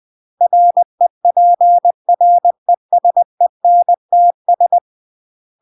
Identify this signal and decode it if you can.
Morse code, heard as a keyed tone, about 20 wpm, 700 Hz. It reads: REPRESENTS
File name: CW / jodorowsky / 82 commandments 20-20